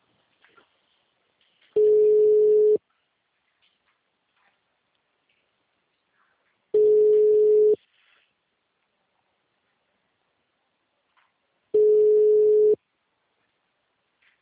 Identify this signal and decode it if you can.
Ringback tone (Europe, 425 Hz).
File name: out- -2041- - - .40118